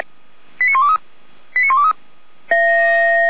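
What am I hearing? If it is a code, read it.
Five-tone selective call (ZVEI), heard as quick runs of five tones, tones of about 70 ms. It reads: 89123 89123
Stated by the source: ZVEI five-tone selective call received on an SDR.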